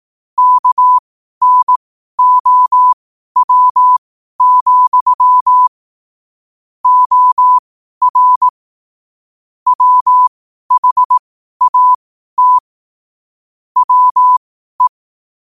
Morse code, heard as a keyed tone, about 18 wpm, spaced as for 12 wpm, 1000 Hz. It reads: KNOW, OR WHAT WE
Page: New quotes every day in morse code at 12 Words per minute.